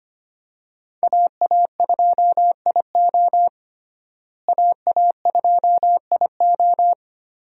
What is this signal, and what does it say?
Morse code, heard as a keyed tone, about 25 wpm, 700 Hz. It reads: AA2IO AA2IO
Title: USA Call Signs - 5 Characters - Repeated 3x - 25wpm